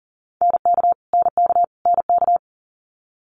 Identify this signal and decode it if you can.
Morse code, heard as a keyed tone, about 41 wpm, 700 Hz. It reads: NK NK NK
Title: Multi-letter Phonemes Repeated 3x 40wpm